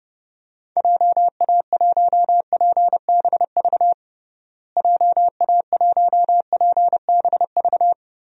Morse code, heard as a keyed tone, about 32 wpm, 700 Hz. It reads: JA1PBV JA1PBV
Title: DXCC-100 Call Signs - 6 Characters - Repeated 2x - 30wpm